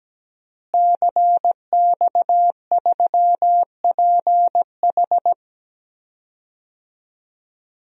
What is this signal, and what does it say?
Morse code, heard as a keyed tone, about 17 wpm, 700 Hz. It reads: CX3PH
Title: DXCC-100 Call Signs - 5 Characters - Repeated 3x - 17wpm